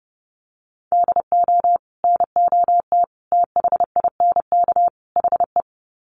Morse code, heard as a keyed tone, about 30 wpm, 700 Hz. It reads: DO NOT THINK HE
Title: Sets of 4 Words 30wpm